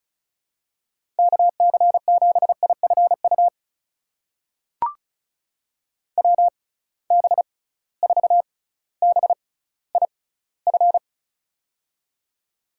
Morse code, heard as a keyed tone, about 36 wpm, spaced as for 18 wpm, 700 Hz. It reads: KC7IFU WB4BIF
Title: USA Call Signs - 6 Characters - 6x Character Spacing - 35wpm